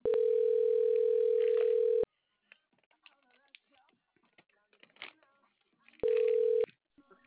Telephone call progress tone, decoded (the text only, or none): ringback tone